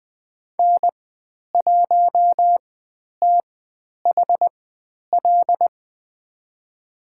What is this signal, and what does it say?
Morse code, heard as a keyed tone, about 20 wpm, spaced as for 10 wpm, 700 Hz. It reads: N1THL
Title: 102 Lesson 17g - Letter u - Call Signs 20wpm